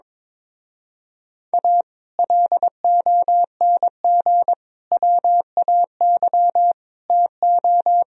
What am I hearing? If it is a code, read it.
Morse code, heard as a keyed tone, about 22 wpm, 700 Hz. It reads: A LONG WAY TO
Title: Speed-Racing - Sets of 5 Words 22wpm